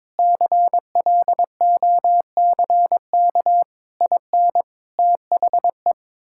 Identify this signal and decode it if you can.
Morse code, heard as a keyed tone, about 22 wpm, 700 Hz. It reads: CLOCK IN THE
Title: Pangram Challenge v2 - 22wpm